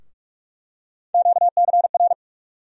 Morse code, heard as a keyed tone, about 46 wpm, 700 Hz. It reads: QCR
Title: USA Call Sign Postfixes - 3 Letters 45wpm